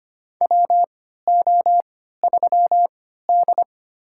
Morse code, heard as a keyed tone, about 25 wpm, spaced as for 14 wpm, 700 Hz. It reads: WO3D
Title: USA Call Signs - 3x Character Spacing - 25wpm